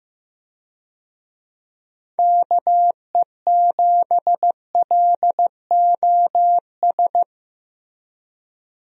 Morse code, heard as a keyed tone, about 15 wpm, 700 Hz. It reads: KE7LOS
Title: USA Call Signs - 6 Characters - Repeated 3x - 15wpm